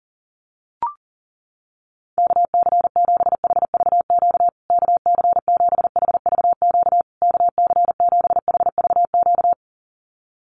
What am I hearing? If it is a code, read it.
Morse code, heard as a keyed tone, about 40 wpm, 700 Hz. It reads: KC7HVQ KC7HVQ KC7HVQ
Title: USA Call Signs Repeated 3x 40wpm